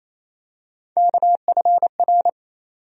Morse code, heard as a keyed tone, about 28 wpm, 700 Hz. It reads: KFR
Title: USA Call Sign Postfixes - 3 Letters 28wpm